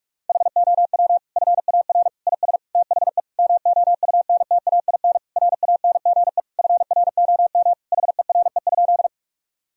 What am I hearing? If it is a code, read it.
Morse code, heard as a keyed tone, about 41 wpm, 700 Hz. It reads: HOW FAR IS THE MOUNTAIN RANGE FROM HERE?
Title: Sentences from Top 1000 Words - Review Part 4 - 45wpm